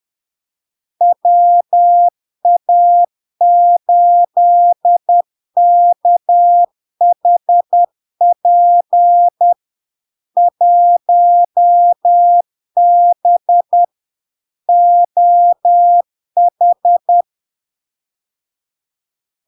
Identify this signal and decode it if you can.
Morse code, heard as a keyed tone, about 10 wpm, 700 Hz. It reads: WA8KHP 1B OH
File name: LARK-FD-2022-Log-in-CW-10wpm-700-Hz